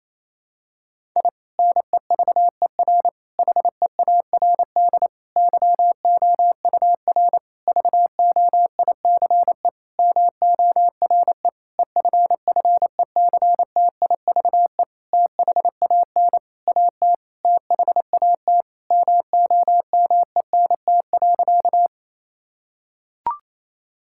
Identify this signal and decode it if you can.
Morse code, heard as a keyed tone, about 28 wpm, 690 Hz. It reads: I NEVER HEARD YOUR VOICE MORE EFFECTIVE THAN AT THAT MOMENT.
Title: Sentences from Top 1000 Words - Review Part 6 - 28wpm